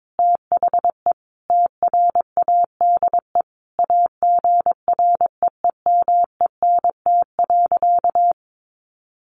Morse code, heard as a keyed tone, about 22 wpm, 700 Hz. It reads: THE TRADE AGREEMENT.
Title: Sentences from Top 1500 Words 22wpm